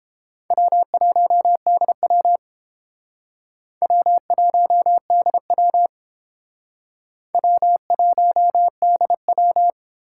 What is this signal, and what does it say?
Morse code, heard as a keyed tone, about 30 wpm, 700 Hz. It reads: W1DW W1DW W1DW